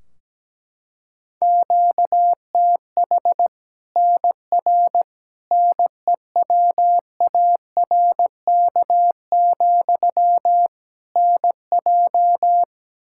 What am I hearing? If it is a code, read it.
Morse code, heard as a keyed tone, about 17 wpm, 700 Hz. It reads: QTH NR NEWARK, NJ
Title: Practice QSO - Exchange 1 - Beginning - 17wpm